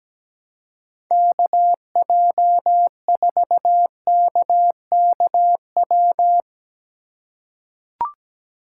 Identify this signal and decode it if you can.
Morse code, heard as a keyed tone, about 17 wpm, 700 Hz. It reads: KJ4KKW
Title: USA Call Signs - 6 Characters - 4x Character Spacing - 17wpm